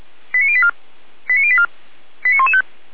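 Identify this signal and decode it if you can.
Five-tone selective call (ZVEI), heard as quick runs of five tones, tones of about 70 ms. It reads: 89084 89084 89115
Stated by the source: ZVEI five-tone call received by SDR.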